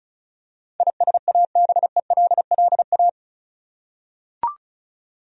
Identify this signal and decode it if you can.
Morse code, heard as a keyed tone, about 36 wpm, 700 Hz. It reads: ISABELLA
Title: USA Common Names - 35wpm